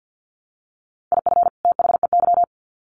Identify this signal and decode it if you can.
Morse code, heard as a keyed tone, about 50 wpm, 700 Hz. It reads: IF THEY
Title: Sets of 2 Words 50wpm